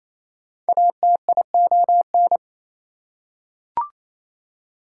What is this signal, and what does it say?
Morse code, heard as a keyed tone, about 28 wpm, 700 Hz. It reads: ATION
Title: Prefixes and Suffixes 28wpm